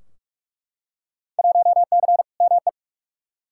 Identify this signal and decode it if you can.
Morse code, heard as a keyed tone, about 47 wpm, 700 Hz. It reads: 1C ME
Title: Field Day Exchanges - 45wpm